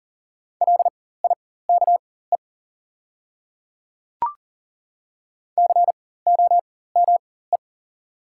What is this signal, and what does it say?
Morse code, heard as a keyed tone, about 40 wpm, spaced as for 19 wpm, 700 Hz. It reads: LIKE COME
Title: 4x Character Spacing - Top 100 Words - 40wpm